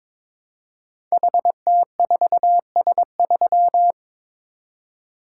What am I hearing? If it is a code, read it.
Morse code, heard as a keyed tone, about 22 wpm, 700 Hz. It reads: HT4S3
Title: Code Groups - 5 Letter:Numbers - 22wpm